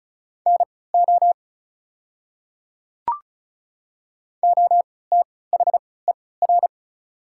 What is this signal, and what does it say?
Morse code, heard as a keyed tone, about 36 wpm, spaced as for 20 wpm, 690 Hz. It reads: NO OTHER
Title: 3x Character Spacing - Top 100 Words - 35wpm